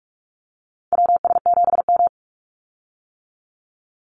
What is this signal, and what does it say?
Morse code, heard as a keyed tone, about 44 wpm, 700 Hz. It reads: WS7M
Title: Call Signs - 45wpm